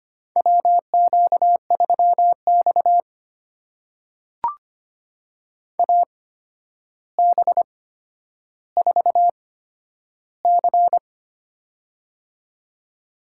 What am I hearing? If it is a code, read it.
Morse code, heard as a keyed tone, about 25 wpm, spaced as for 11 wpm, 700 Hz. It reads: WQ3X AB4C
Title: USA Call Signs - 8x Character Spacing - 25wpm